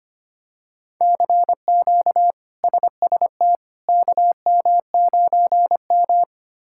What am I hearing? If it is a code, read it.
Morse code, heard as a keyed tone, about 25 wpm, 700 Hz. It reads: CQ SST KM9M
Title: K1USN SST QSO Practice - 25wpm